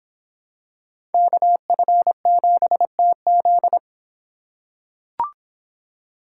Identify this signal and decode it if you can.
Morse code, heard as a keyed tone, about 25 wpm, 700 Hz. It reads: KF7TZ